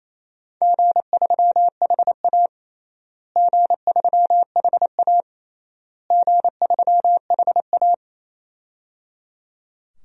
Morse code, heard as a keyed tone, about 28 wpm, 700 Hz. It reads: G3HA G3HA G3HA
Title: DXCC-100 Call Signs - 4 Characters - Repeated 3x - 28wpm